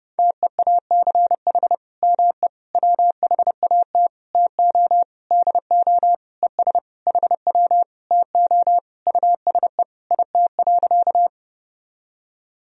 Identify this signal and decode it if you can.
Morse code, heard as a keyed tone, about 32 wpm, 700 Hz. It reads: TEACH ME WHAT TO DO ES HW TO USE IT.
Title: Sentences from Top 500 Words and Abbreviations 30wpm